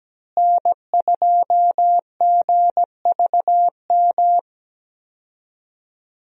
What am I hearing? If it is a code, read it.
Morse code, heard as a keyed tone, about 17 wpm, 700 Hz. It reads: N2GVM
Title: USA Call Signs - 5 Characters - Repeated 2x - 17wpm